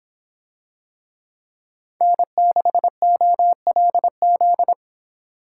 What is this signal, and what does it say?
Morse code, heard as a keyed tone, about 26 wpm, 700 Hz. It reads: N6OLZ